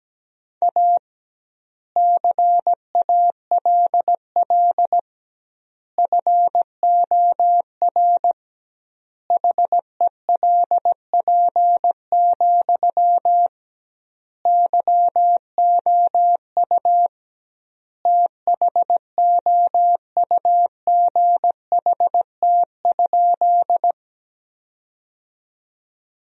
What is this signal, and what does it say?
Morse code, heard as a keyed tone, about 17 wpm, 700 Hz. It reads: A CALL FOR HELP, YOU THOUGHT?
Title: 2x Word-Spacing - Sentences from Top 300 Words - 17wpm